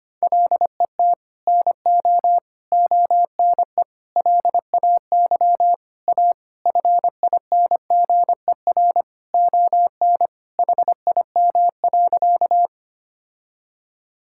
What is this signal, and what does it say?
Morse code, heard as a keyed tone, about 25 wpm, 700 Hz. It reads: LET NO ONE LAY A FINGER ON HIM.
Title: Sentences from Top 1600 Words 25wpm